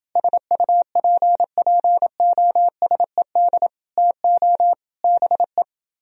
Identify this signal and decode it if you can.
Morse code, heard as a keyed tone, about 29 wpm, 700 Hz. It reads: SUPPOSED TO BE